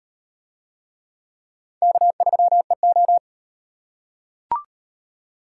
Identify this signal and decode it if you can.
Morse code, heard as a keyed tone, about 38 wpm, 700 Hz. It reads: K3EO